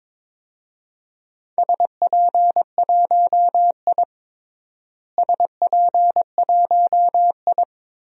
Morse code, heard as a keyed tone, about 22 wpm, 700 Hz. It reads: SP1I SP1I
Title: DXCC-100 Call Signs - 4 Characters - Repeated 2x - 22wpm